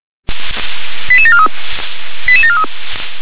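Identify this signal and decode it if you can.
Five-tone selective call (ZVEI), heard as quick runs of five tones, tones of about 70 ms. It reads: 88743 88743
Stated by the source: ZVEI five-tone selective call received on SDR.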